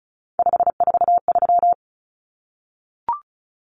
Morse code, heard as a keyed tone, about 34 wpm, 700 Hz. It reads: 543
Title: Numbers - 3-Digit 35wpm